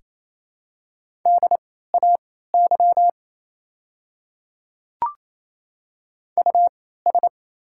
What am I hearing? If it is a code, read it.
Morse code, heard as a keyed tone, about 27 wpm, spaced as for 16 wpm, 690 Hz. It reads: DAY US
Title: 3x Character Spacing - Top 200 Words - 28wpm